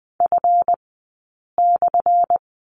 Morse code, heard as a keyed tone, about 20 wpm, spaced as for 8 wpm, 700 Hz. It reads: F/
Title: 177 Lesson 29d - Slash - Two Characters 20wpm